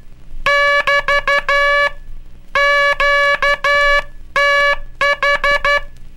Morse code, heard as a keyed tone, about 12 wpm, 1700 Hz. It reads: <BT> QTH
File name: cursos / Curso CW mp3 / CW Aula - 18